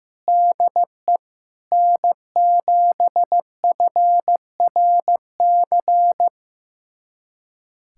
Morse code, heard as a keyed tone, about 15 wpm, 700 Hz. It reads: DE N7FRC
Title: USA Callsign DE Callsign - 5 Characters - 15wpm